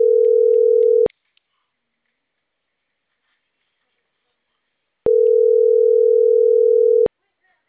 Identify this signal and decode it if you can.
Ringback tone (North America), 440 + 480 Hz.